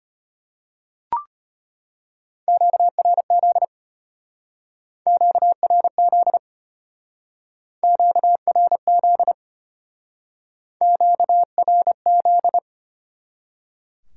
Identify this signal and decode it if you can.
Morse code, heard as a keyed tone, about 32 wpm, 700 Hz. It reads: QRZ QRZ QRZ QRZ